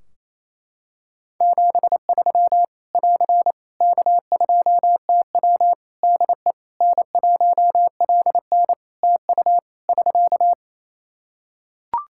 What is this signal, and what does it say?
Morse code, heard as a keyed tone, about 28 wpm, 700 Hz. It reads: 73 <AR> K2TW DE N1LN TU <SK>
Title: Practice QSO - Exchange 4 - Ending - 28wpm